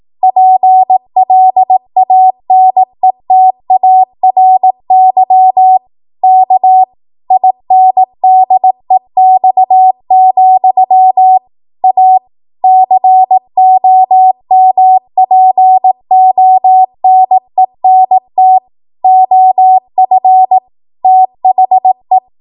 Morse code, heard as a keyed tone, about 18 wpm, 750 Hz. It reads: PLANETARY K INDEX, A COMPONENT OF THE
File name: Archive / 18 WPM / 18WPM